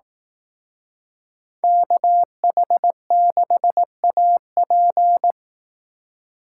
Morse code, heard as a keyed tone, about 18 wpm, 700 Hz. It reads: KH6AP
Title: Speed Racing - USA 2x2 Call Signs 15wpm